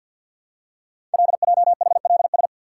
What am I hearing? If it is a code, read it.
Morse code, heard as a keyed tone, about 50 wpm, 700 Hz. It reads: LJHLS
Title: Code Groups - 5 Letters - 50wpm